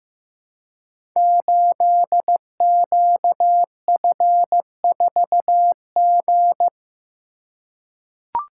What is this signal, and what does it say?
Morse code, heard as a keyed tone, about 15 wpm, 700 Hz. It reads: 8QF4G
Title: Code Groups - 5 Letter:Numbers - 15wpm